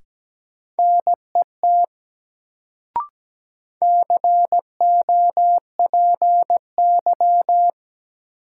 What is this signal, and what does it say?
Morse code, heard as a keyed tone, about 17 wpm, 700 Hz. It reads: NET COPY
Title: Sentences from Top 1400 Words - New Word Drill Alt - 17wpm